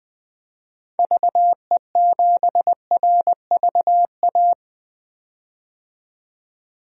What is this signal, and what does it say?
Morse code, heard as a keyed tone, about 20 wpm, 700 Hz. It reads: VE7RVA
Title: DXCC-100 Call Signs - 6 Characters - 20wpm